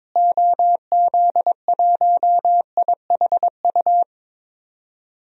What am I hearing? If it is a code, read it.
Morse code, heard as a keyed tone, about 22 wpm, 700 Hz. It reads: OZ1IHU
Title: DXCC-100 Call Signs - 6 Characters - Repeated 2x - 22wpm